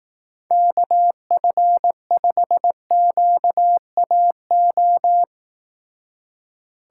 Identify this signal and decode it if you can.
Morse code, heard as a keyed tone, about 18 wpm, 700 Hz. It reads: KF5QAO
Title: Speed Racing - USA 2x3 Call Signs 15wpm